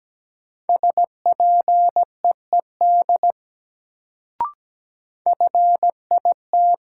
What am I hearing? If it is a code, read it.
Morse code, heard as a keyed tone, about 17 wpm, 700 Hz. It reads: SPEED FIT
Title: Sentences from Top 800 Words - New Word Drill - 17wpm